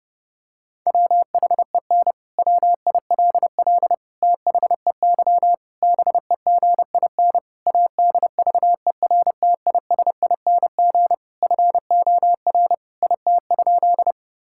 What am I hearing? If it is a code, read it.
Morse code, heard as a keyed tone, about 29 wpm, 700 Hz. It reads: WHEN WILL THEY BEGIN ADVERTISING FOR IT?
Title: Sentences from Top 1000 Words - Review Part 6 - 30wpm